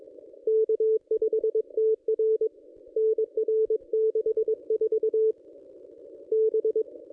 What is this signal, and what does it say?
Morse code, heard as a keyed tone, about 22 wpm, 440 Hz. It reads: K5TR NR 64 B
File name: contests / ss-cw-k5tr-op-n5rz -1400